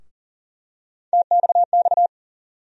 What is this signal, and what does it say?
Morse code, heard as a keyed tone, about 40 wpm, 700 Hz. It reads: TXX
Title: USA Call Sign Postfixes - 3 Letters 40wpm